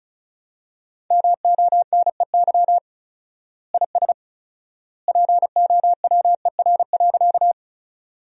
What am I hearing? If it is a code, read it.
Morse code, heard as a keyed tone, about 34 wpm, 700 Hz. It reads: MONEY IS POWER.
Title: 4x Word-Spacing - Sentences from Top 500 Words - 35wpm